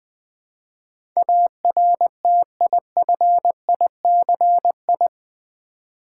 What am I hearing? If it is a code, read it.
Morse code, heard as a keyed tone, about 20 wpm, 700 Hz. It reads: ARTIFICI
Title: Build A Word - 4 Letters At A Time - 10-Letter Words - 20wpm